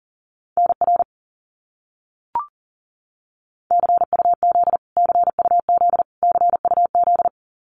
Morse code, heard as a keyed tone, about 39 wpm, 700 Hz. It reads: NR CUZ CUZ CUZ
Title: Abbreviations Repeated 3x - 40wpm